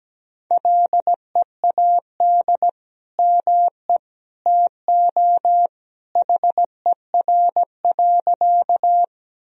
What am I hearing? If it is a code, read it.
Morse code, heard as a keyed tone, about 17 wpm, 700 Hz. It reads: LEAD ME TO HER.
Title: Sentences from Top 1000 Words - Review Part 2 - 17wpm